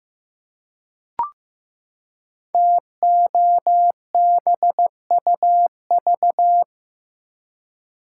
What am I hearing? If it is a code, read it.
Morse code, heard as a keyed tone, about 15 wpm, 700 Hz. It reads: TOBUV
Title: Code Groups - 5 Letters - 15wpm